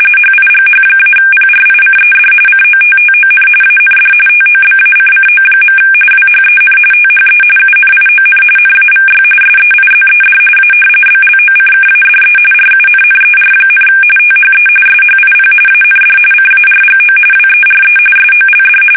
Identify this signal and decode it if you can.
RTTY, about 45.45 baud. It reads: HIGH-PRICED COTTON AND TOBACCO. WHEN THE UNION NAVY SEIZED A BLOCKADE RUNNER, THE SHIP AND CARGO WERE SOLD
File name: fsk850 / usnavy3